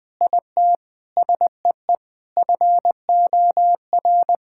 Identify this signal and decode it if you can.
Morse code, heard as a keyed tone, about 20 wpm, 700 Hz. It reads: IT SEE FOR
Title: Sentences with 8-Word Phrases - 20wpm